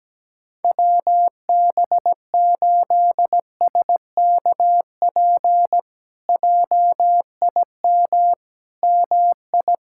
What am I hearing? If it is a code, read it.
Morse code, heard as a keyed tone, about 17 wpm, 700 Hz. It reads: WB8SKP JIM MI
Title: K1USN SST QSO Practice - 17wpm